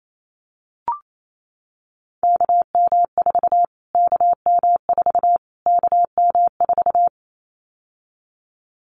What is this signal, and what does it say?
Morse code, heard as a keyed tone, about 28 wpm, 700 Hz. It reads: KM4 KM4 KM4
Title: Callsign Prefixes for Top 25 DXCC - Repeated 3x - 28wpm